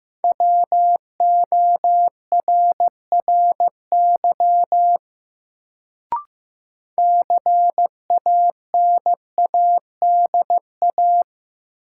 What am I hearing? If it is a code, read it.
Morse code, heard as a keyed tone, about 15 wpm, 700 Hz. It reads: WORRY CANADA
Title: Sets of 5 Words - New Word Drill - 15wpm